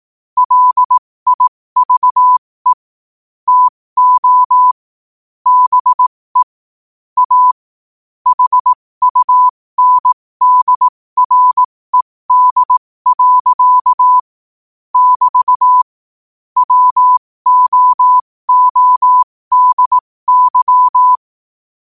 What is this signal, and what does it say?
Morse code, heard as a keyed tone, about 18 wpm, spaced as for 15 wpm, 1000 Hz. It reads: LIVE TO BE A HUNDRED. <BT> WOODY
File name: qotdfiles / 15wpm